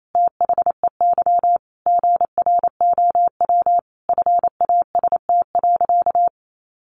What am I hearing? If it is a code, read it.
Morse code, heard as a keyed tone, about 28 wpm, 700 Hz. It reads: THEY GROW FAST.
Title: Sentences from Top 1300 Words 28wpm